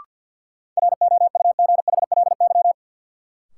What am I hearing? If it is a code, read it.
Morse code, heard as a keyed tone, about 52 wpm, 700 Hz. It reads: ROUGHLY
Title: Sentences from Top 1500 Words - New Word Drill - 50wpm